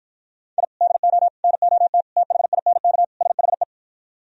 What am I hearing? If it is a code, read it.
Morse code, heard as a keyed tone, about 53 wpm, 700 Hz. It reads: I DO NOT THINK SHE